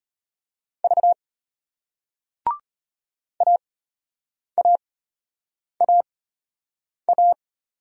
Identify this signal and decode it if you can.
Morse code, heard as a keyed tone, about 34 wpm, spaced as for 8 wpm, 700 Hz. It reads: V AAAA